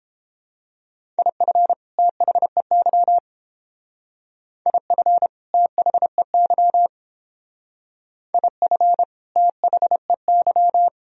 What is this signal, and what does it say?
Morse code, heard as a keyed tone, about 28 wpm, 700 Hz. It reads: IF THEY IF THEY IF THEY
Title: Speed-Racing - Sets of 2 Words 22wpm